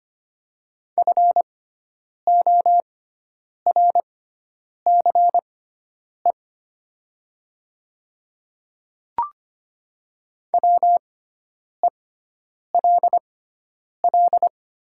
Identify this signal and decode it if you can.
Morse code, heard as a keyed tone, about 25 wpm, spaced as for 9 wpm, 700 Hz. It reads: FORCE WELL
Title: 6x Character Spacing - Top 500 Words - 25wpm